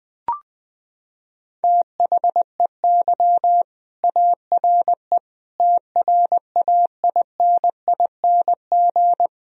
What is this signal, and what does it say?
Morse code, heard as a keyed tone, about 20 wpm, 700 Hz. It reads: THEY ARE TRAINING
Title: Sentences from Top 1000 Words - Review Part 4 - 20wpm